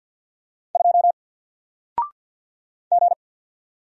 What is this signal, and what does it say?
Morse code, heard as a keyed tone, about 48 wpm, 700 Hz. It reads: 2 G
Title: Single Letter-Number Alt 50wpm